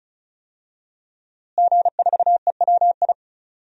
Morse code, heard as a keyed tone, about 34 wpm, 700 Hz. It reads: G4EWI
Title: DXCC-100 Call Signs - 5 Characters - Repeated 2x - 35wpm